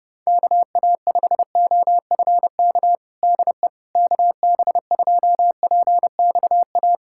Morse code, heard as a keyed tone, about 30 wpm, 700 Hz. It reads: KA5OFK DE KB2PXA
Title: USA Callsign DE Callsign - 6 Characters - 30wpm